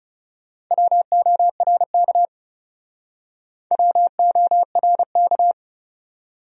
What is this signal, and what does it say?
Morse code, heard as a keyed tone, about 38 wpm, 700 Hz. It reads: WORK WORK